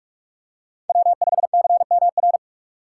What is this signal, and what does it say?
Morse code, heard as a keyed tone, about 41 wpm, 700 Hz. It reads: W5CMR